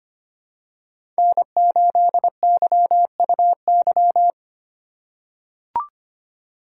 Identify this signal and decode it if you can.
Morse code, heard as a keyed tone, about 25 wpm, 700 Hz. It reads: N8YUY
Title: USA Call Signs - 5 Characters - 4x Character Spacing - 25wpm